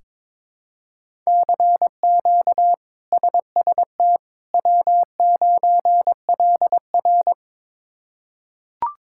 Morse code, heard as a keyed tone, about 22 wpm, 700 Hz. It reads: CQ SST W9LR
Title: K1USN SST QSO Practice - 22wpm